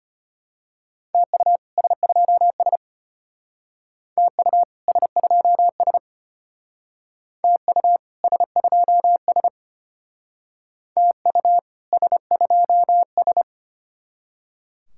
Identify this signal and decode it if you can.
Morse code, heard as a keyed tone, about 32 wpm, 700 Hz. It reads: TU S2S TU S2S TU S2S TU S2S